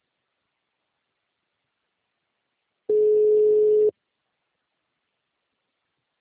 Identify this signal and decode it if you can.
Ringback tone (Europe, 425 Hz).
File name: out- -2001- - - .35315